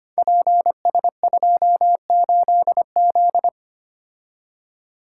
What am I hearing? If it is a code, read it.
Morse code, heard as a keyed tone, about 25 wpm, 700 Hz. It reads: PS28Z
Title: Code Groups - 5 Letter:Numbers - 25wpm